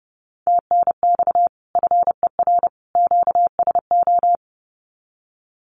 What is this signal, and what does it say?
Morse code, heard as a keyed tone, about 30 wpm, 700 Hz. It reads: TNX FER QSO
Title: Common QSO Phrases 30wpm